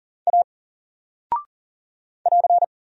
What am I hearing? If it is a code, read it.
Morse code, heard as a keyed tone, about 39 wpm, 700 Hz. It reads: A <AR>
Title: Single Character (Letter:Number:Punctuation:Prosign) 40wpm